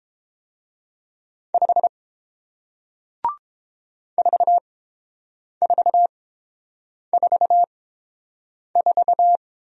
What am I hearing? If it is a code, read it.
Morse code, heard as a keyed tone, about 31 wpm, spaced as for 7 wpm, 700 Hz. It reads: 5 4444